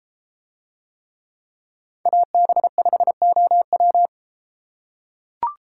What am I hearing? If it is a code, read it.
Morse code, heard as a keyed tone, about 33 wpm, 700 Hz. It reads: AB5OW